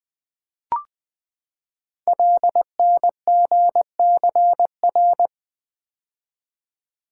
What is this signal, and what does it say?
Morse code, heard as a keyed tone, about 20 wpm, 700 Hz. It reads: LNGCR
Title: Code Groups - 5 Letters - 20wpm